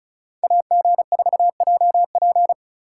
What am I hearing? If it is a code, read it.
Morse code, heard as a keyed tone, about 34 wpm, 700 Hz. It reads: AG4JP
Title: USA Call Signs - 5 Characters - 2x Character Spacing - 35wpm